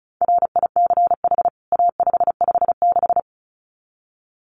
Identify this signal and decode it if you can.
Morse code, heard as a keyed tone, about 34 wpm, 700 Hz. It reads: RICH A556
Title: CWT Exchange 35wpm